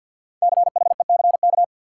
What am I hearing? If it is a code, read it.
Morse code, heard as a keyed tone, about 46 wpm, 700 Hz. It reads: CHECK
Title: Sentences from Top 1400 Words - New Word Drill Alt - 50wpm